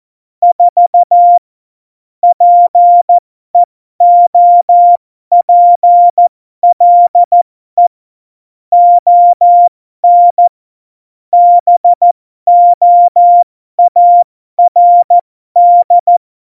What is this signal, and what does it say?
Morse code, heard as a keyed tone, about 14 wpm, spaced as for 12 wpm, 700 Hz. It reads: 4 PEOPLE ON BOARD
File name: Morsekurs / Witze